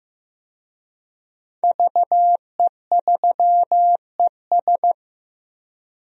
Morse code, heard as a keyed tone, about 15 wpm, 700 Hz. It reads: VE3ES
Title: DXCC-100 Call Signs - 5 Characters - 15wpm